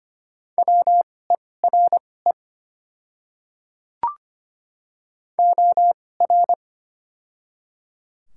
Morse code, heard as a keyed tone, about 25 wpm, spaced as for 18 wpm, 700 Hz. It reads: WERE OR
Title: 2x Character Spacing - Top 200 Words - 25wpm